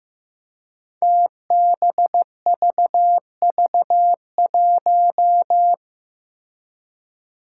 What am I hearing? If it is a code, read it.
Morse code, heard as a keyed tone, about 15 wpm, 700 Hz. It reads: TBVV1
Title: Five Challenging Letter-Number Combinations 15wpm